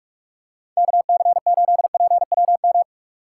Morse code, heard as a keyed tone, about 46 wpm, 700 Hz. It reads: KC8PWM